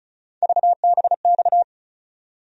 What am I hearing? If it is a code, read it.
Morse code, heard as a keyed tone, about 36 wpm, 700 Hz. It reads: VBX
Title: USA Call Sign Postfixes - 3 Letters 35wpm